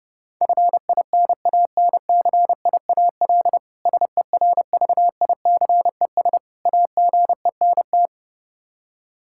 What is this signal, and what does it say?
Morse code, heard as a keyed tone, about 30 wpm, 700 Hz. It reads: FINANCIAL SERVICES AGENT
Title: K1USN SST QSO Practice 30wpm